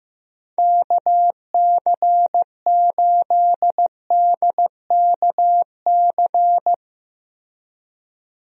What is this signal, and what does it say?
Morse code, heard as a keyed tone, about 15 wpm, 700 Hz. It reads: KC8DKC
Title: USA Call Signs - 6 Characters - Repeated 3x - 15wpm